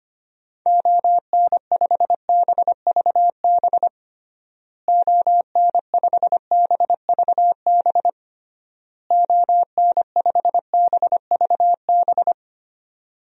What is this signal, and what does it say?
Morse code, heard as a keyed tone, about 25 wpm, 700 Hz. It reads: ON5BVB ON5BVB ON5BVB
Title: DXCC-100 Call Signs - 6 Characters - Repeated 3x - 25wpm